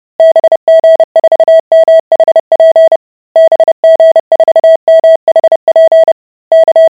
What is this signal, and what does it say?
Morse code, heard as a keyed tone, about 30 wpm, 650 Hz. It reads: BG4MHP BG4MHP K